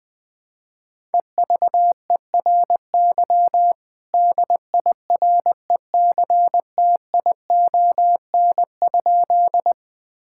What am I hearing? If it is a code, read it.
Morse code, heard as a keyed tone, about 21 wpm, 700 Hz. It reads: EVERY DIRECTION?
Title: Sentences from Top 1000 Words - Review Part 5 - 20wpm